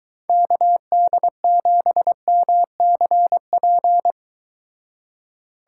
Morse code, heard as a keyed tone, about 23 wpm, 700 Hz. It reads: KD7MCP